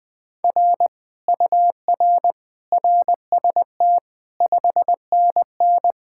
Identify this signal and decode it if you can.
Morse code, heard as a keyed tone, about 20 wpm, 700 Hz. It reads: R UR RST 5NN
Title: Speed-Racing Common QSO Phrases Encore - 17wpm